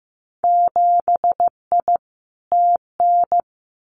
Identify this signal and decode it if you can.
Morse code, heard as a keyed tone, about 15 wpm, 700 Hz. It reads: 7I TN
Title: Winter Field Day Exchanges Encore 15wpm